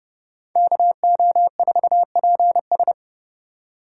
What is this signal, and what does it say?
Morse code, heard as a keyed tone, about 27 wpm, 700 Hz. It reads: KO4PS